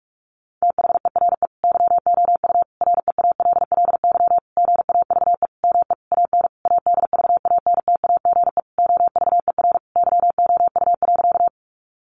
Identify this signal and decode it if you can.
Morse code, heard as a keyed tone, about 44 wpm, 700 Hz. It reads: THERE YOU REALLY GAVE ME AN ADVANTAGE OVER YOU.
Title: Sentences from Top 900 Words 45wpm